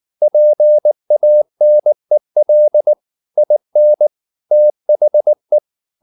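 Morse code, heard as a keyed tone, about 19 wpm, 600 Hz. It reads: PANEL IN THE
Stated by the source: Morse code at 19 wpm from CW med Gnister, reading Ken Follett's novel Never in English.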